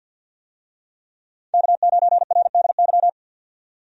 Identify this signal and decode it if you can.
Morse code, heard as a keyed tone, about 52 wpm, 700 Hz. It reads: K9RDY
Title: USA Call Signs - 5 Characters - 50wpm